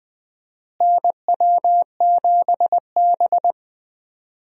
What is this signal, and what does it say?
Morse code, heard as a keyed tone, about 20 wpm, 700 Hz. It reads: NW7B
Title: USA Call Signs - 4 Characters - Repeated 2x - 20wpm